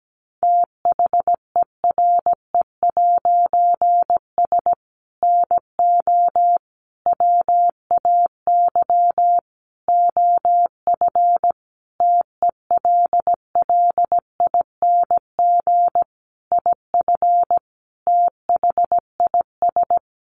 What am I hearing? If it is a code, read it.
Morse code, heard as a keyed tone, about 17 wpm, 700 Hz. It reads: THERE'S NO WAY OF TELLING IF THIS
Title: A Sound Of Thunder 17wpm